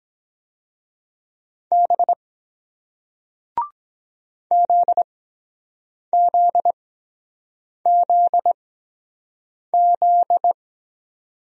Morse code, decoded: B ZZZZ

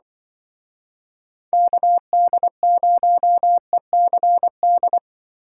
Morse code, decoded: KD0ECD

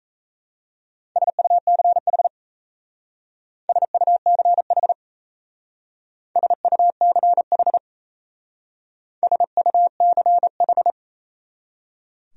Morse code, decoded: SUCH SUCH SUCH SUCH

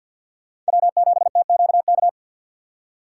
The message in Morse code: W7TQK

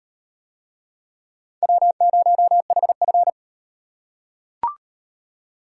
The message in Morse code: W0HF

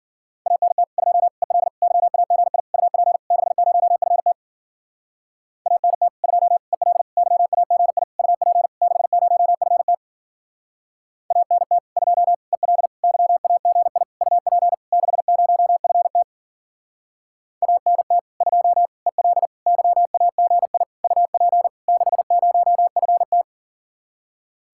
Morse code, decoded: ANT 2 EL YAGI UP 60FT ANT 2 EL YAGI UP 60FT ANT 2 EL YAGI UP 60FT ANT 2 EL YAGI UP 60FT

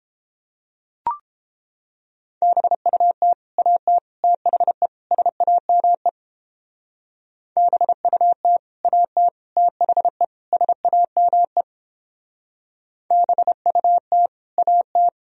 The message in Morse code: BUT AT THE SAME BUT AT THE SAME BUT AT

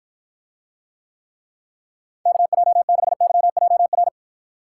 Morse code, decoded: KJ6YJR

